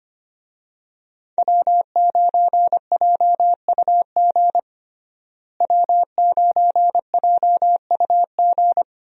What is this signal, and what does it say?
Morse code, heard as a keyed tone, about 25 wpm, 700 Hz. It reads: W9JUG W9JUG
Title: USA Call Signs - 5 Characters - Repeated 3x - 25wpm